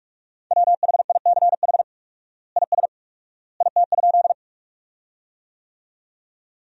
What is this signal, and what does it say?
Morse code, heard as a keyed tone, about 46 wpm, 700 Hz. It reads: WHICH IS IT?
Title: 4x Word-Spacing - Sentences from Top 100 Words - 45wpm